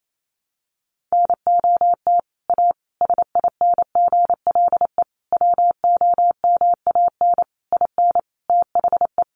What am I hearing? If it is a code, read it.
Morse code, heard as a keyed tone, about 28 wpm, 700 Hz. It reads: NOT A SINGLE WOMAN IN THE
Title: Sentences from Top 1100 Words 28wpm